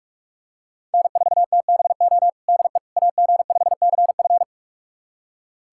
Morse code, heard as a keyed tone, about 46 wpm, 700 Hz. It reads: N4TBO DE AG5CF
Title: USA Callsign DE Callsign - 5 Characters - 45wpm